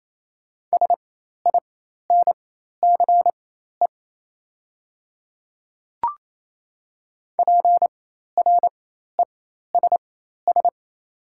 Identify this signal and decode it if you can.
Morse code, heard as a keyed tone, about 28 wpm, spaced as for 13 wpm, 700 Hz. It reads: SINCE PRESS
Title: 4x Character Spacing - Top 300 Words - 28wpm